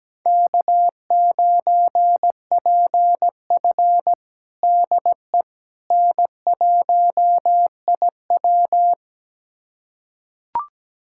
Morse code, heard as a keyed tone, about 17 wpm, 700 Hz. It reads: K9PF DE N1IW
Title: USA Callsign DE Callsign - 4 Characters - 17wpm